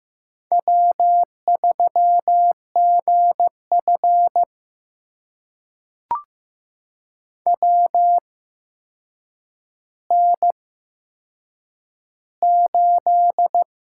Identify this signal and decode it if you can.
Morse code, heard as a keyed tone, about 15 wpm, spaced as for 7 wpm, 700 Hz. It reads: W3GF WN8
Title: USA Call Signs - 4 Characters - 8x Character Spacing - 15wpm